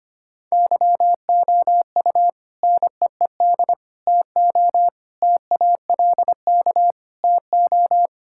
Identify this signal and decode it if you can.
Morse code, heard as a keyed tone, about 25 wpm, 700 Hz. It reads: YOU NEED TO TALK TO
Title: Sentences from Top 1600 Words 25wpm